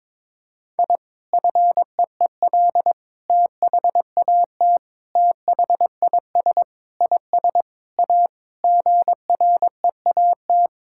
Morse code, heard as a keyed tone, about 22 wpm, 700 Hz. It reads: I FEEL THAT THIS IS A GREAT
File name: practice / 22wpm / Sentences with 4-Word Phrases - 22wpm